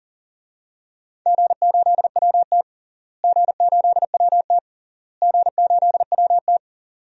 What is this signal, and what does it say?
Morse code, heard as a keyed tone, about 40 wpm, 700 Hz. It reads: G8WT G8WT G8WT
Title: DXCC-100 Call Signs - 4 Characters - Repeated 3x - 40wpm